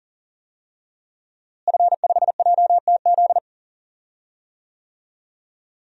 Morse code, heard as a keyed tone, about 40 wpm, 700 Hz. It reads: F5JTZ
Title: DXCC-100 Call Signs - 5 Characters - Repeated 2x - 40wpm